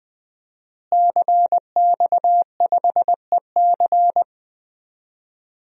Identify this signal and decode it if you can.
Morse code, heard as a keyed tone, about 20 wpm, 700 Hz. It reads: CX5EC
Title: DXCC-100 Call Signs - 5 Characters - Repeated 3x - 20wpm